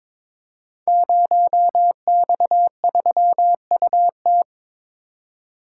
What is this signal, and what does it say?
Morse code, heard as a keyed tone, about 22 wpm, 700 Hz. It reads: 0X3UT
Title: Code Groups - 5 Letter:Numbers - 22wpm